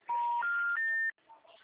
Special information tone.